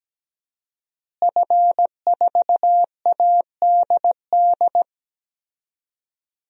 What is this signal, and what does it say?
Morse code, heard as a keyed tone, about 17 wpm, 700 Hz. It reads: F4ADD
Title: DXCC-100 Call Signs - 5 Characters - Repeated 2x - 17wpm